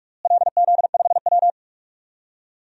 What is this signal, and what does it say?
Morse code, heard as a keyed tone, about 46 wpm, 700 Hz. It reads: LZ5W
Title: DXCC-100 Call Signs - 4 Characters - Repeated 2x - 45wpm